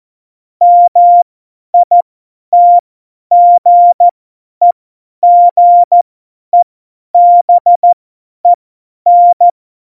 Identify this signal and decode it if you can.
Morse code, heard as a keyed tone, about 14 wpm, spaced as for 10 wpm, 700 Hz. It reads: MITGEGEBEN
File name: Morsekurs / Witze / Witz 22. September 2020, 50 F70